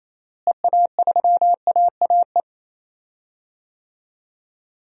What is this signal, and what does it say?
Morse code, heard as a keyed tone, about 28 wpm, 700 Hz. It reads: EA3AAE
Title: DXCC-100 Call Signs - 6 Characters - Repeated 3x - 28wpm